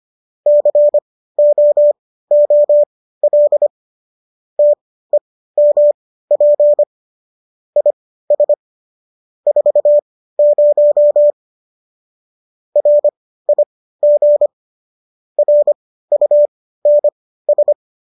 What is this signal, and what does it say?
Morse code, heard as a keyed tone, about 25 wpm, spaced as for 15 wpm, 590 Hz. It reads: COOL TEMP IS 40 RIG RUNS
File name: qso201 15